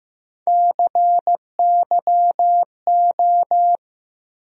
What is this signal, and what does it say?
Morse code, heard as a keyed tone, about 15 wpm, 700 Hz. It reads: CYO
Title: USA Call Sign Postfixes - 3 Letters 15wpm